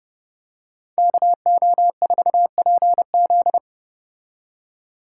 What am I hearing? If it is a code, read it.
Morse code, heard as a keyed tone, about 30 wpm, 700 Hz. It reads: KO4PZ